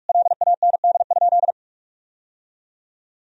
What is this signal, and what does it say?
Morse code, heard as a keyed tone, about 46 wpm, 700 Hz. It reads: LAND?
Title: Sentences from Top 1000 Words - Review Part 1 - 45wpm